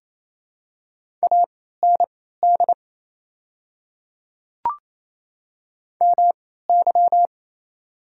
Morse code, heard as a keyed tone, about 27 wpm, spaced as for 16 wpm, 690 Hz. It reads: AND MY
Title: 3x Character Spacing - Top 100 Words - 28wpm